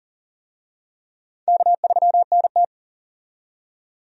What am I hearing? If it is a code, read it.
Morse code, heard as a keyed tone, about 40 wpm, 700 Hz. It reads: K3NT
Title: USA Call Signs - 4 Characters - 40wpm